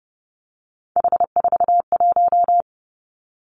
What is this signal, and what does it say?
Morse code, heard as a keyed tone, about 30 wpm, 700 Hz. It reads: H41
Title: Three Challenging Letter-Number Combinations 30wpm